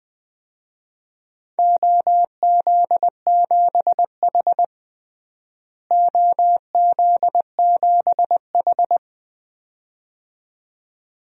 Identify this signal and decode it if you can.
Morse code, heard as a keyed tone, about 21 wpm, 700 Hz. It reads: OZ7H OZ7H
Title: DXCC-100 Call Signs - 4 Characters - Repeated 2x - 20wpm